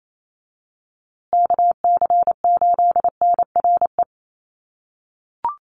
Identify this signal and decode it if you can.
Morse code, heard as a keyed tone, about 28 wpm, 700 Hz. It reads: KC8NRE
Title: USA Call Signs Repeated 3x 28wpm